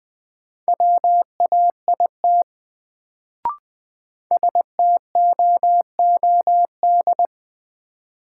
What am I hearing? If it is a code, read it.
Morse code, heard as a keyed tone, about 20 wpm, 700 Hz. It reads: WAIT STOOD
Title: Sentences from Top 500 Words - New Word Drill - 20wpm